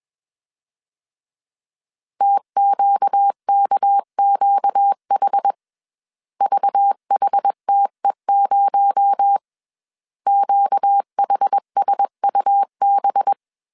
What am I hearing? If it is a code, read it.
Morse code, heard as a keyed tone, about 21 wpm, 800 Hz. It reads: TQKQH VHTE0 QHSUB
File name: CW / DET / 1 6 1500C S A 20wpm 2048